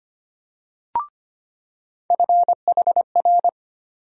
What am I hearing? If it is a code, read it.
Morse code, heard as a keyed tone, about 25 wpm, 700 Hz. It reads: FHR
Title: USA Call Sign Postfixes - 3 Letters 25wpm